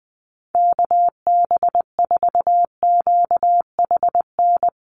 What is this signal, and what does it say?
Morse code, heard as a keyed tone, about 20 wpm, 700 Hz. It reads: KB4QHN
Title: USA Call Signs Repeated 3x 20wpm